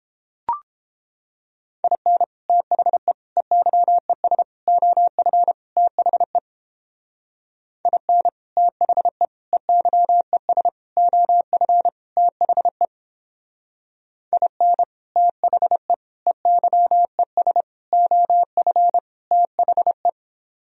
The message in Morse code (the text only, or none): IN THE EYES OF THE IN THE EYES OF THE IN THE EYES OF THE